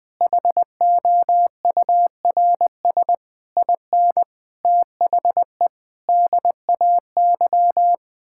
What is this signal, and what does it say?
Morse code, heard as a keyed tone, about 20 wpm, 700 Hz. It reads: HOURS IN THE DAY
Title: Sets of 9 Words - 20wpm